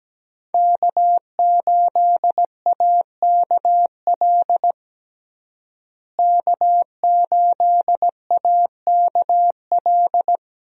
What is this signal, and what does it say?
Morse code, heard as a keyed tone, about 17 wpm, 700 Hz. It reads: K8AKL K8AKL
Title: USA Call Signs - 5 Characters - Repeated 2x - 17wpm